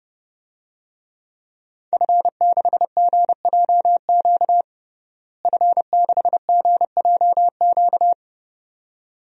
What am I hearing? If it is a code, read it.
Morse code, heard as a keyed tone, about 30 wpm, 700 Hz. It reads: F6GJQ F6GJQ
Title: DXCC-100 Call Signs - 5 Characters - Repeated 2x - 30wpm